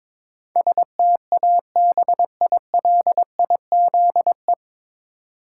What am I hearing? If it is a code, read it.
Morse code, heard as a keyed tone, about 22 wpm, 700 Hz. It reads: STABILIZE
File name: 9 Letter Words - First 3 Letters Given - 22wpm